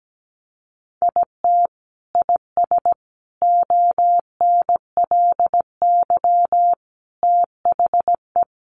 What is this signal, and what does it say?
Morse code, heard as a keyed tone, about 17 wpm, 700 Hz. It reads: IT IS ONLY THE
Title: Sentences from Top 800 Words 17wpm